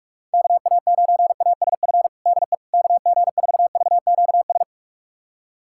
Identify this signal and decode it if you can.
Morse code, heard as a keyed tone, about 42 wpm, 700 Hz. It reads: KA9ASF DE KG4VQS
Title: USA Callsign DE Callsign 45wpm